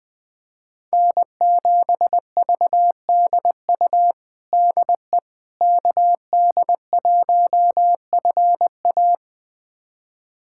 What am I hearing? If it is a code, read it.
Morse code, heard as a keyed tone, about 20 wpm, 700 Hz. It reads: N7VDU DE KD1FA
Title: USA Callsign DE Callsign - 5 Characters - 20wpm